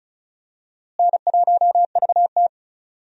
Morse code, heard as a keyed tone, about 36 wpm, 700 Hz. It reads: N1VT
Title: USA Call Signs - 4 Characters - Repeated 2x - 35wpm